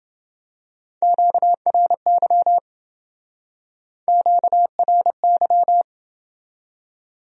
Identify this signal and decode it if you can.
Morse code, heard as a keyed tone, about 28 wpm, 700 Hz. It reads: QRY QRY